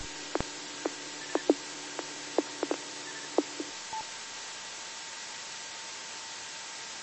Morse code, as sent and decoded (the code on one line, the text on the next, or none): none